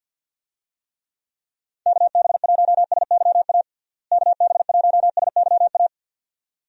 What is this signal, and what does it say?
Morse code, heard as a keyed tone, about 51 wpm, 700 Hz. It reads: KB1SYA KB1SYA
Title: USA Call Signs - 6 Characters - Repeated 2x - 50wpm